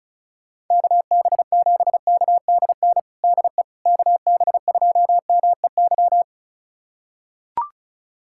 Morse code, heard as a keyed tone, about 34 wpm, 700 Hz. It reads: KB7KDN DE KB2MEY
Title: USA Callsign DE Callsign - 6 Characters - 35wpm